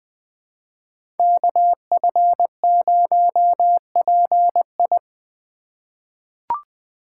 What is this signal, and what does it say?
Morse code, heard as a keyed tone, about 20 wpm, 700 Hz. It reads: KF0PI
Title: USA Call Signs - 5 Characters - 4x Character Spacing - 20wpm